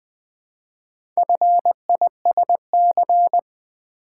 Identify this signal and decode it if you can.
Morse code, heard as a keyed tone, about 20 wpm, 700 Hz. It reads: FISC
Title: Build A Word - 2 Letters At A Time - 6-Letter Words - 20wpm